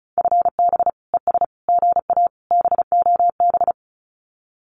Morse code, heard as a keyed tone, about 36 wpm, 700 Hz. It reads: FB ES GA BOB
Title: Common QSO Phrases Encore 35wpm